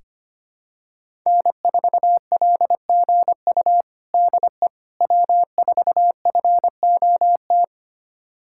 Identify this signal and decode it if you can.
Morse code, heard as a keyed tone, about 25 wpm, 700 Hz. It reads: N4LGU DE W4FOT
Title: USA Callsign DE Callsign - 5 Characters - 25wpm